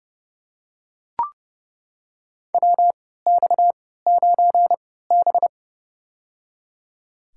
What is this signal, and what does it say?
Morse code, heard as a keyed tone, about 30 wpm, spaced as for 17 wpm, 700 Hz. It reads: WX9B
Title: USA Call Signs - 4 Characters - 3x Character Spacing - 30wpm